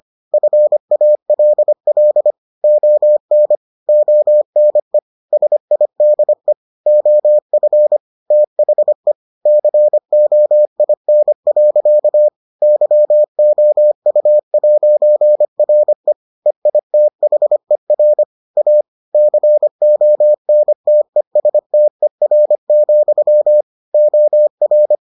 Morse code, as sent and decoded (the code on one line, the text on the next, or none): ..-. .- .-.. .-.. / --- -. / --- -. / ... .. -.. / --- ..-. / - .... / -.-. --- .. -. .-.-.- / -.-- --- ..- .----. .-. / . .. - .... .-. / .- / -.-. --- -. - . ... - . .-. --..-- / --- .-.
FALL ON ONE SIDE OF THE COIN. YOU'RE EITHER A CONTESTER, OR